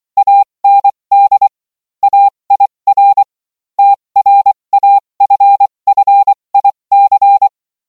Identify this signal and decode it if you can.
Morse code, heard as a keyed tone, about 24 wpm, spaced as for 20 wpm, 800 Hz. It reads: AND AIR TRAFFIC